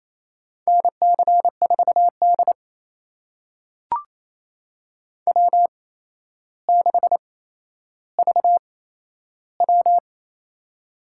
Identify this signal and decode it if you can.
Morse code, heard as a keyed tone, about 27 wpm, spaced as for 12 wpm, 700 Hz. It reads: NC4D W6VW
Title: USA Call Signs - 4 Characters - 8x Character Spacing - 28wpm